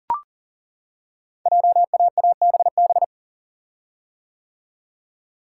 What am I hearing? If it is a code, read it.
Morse code, heard as a keyed tone, about 42 wpm, 700 Hz. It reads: JAABB
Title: Code Groups - 5 Letters - 40wpm